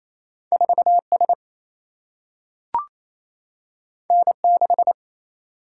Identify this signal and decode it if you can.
Morse code, heard as a keyed tone, about 28 wpm, 700 Hz. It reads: 4 S N 6
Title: Single-Letter Single-Number 28wpm